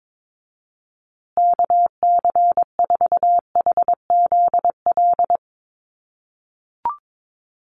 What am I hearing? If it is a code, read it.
Morse code, heard as a keyed tone, about 22 wpm, 700 Hz. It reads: KC4HZL
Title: USA Call Signs Repeated 3x 22wpm